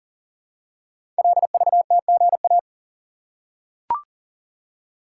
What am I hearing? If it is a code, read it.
Morse code, heard as a keyed tone, about 39 wpm, 690 Hz. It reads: LVTGA
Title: Code Groups - 5 Letters - 40wpm